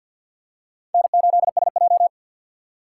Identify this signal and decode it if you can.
Morse code, heard as a keyed tone, about 54 wpm, 700 Hz. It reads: N8SJ
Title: USA Call Signs - 3x Character Spacing - 50wpm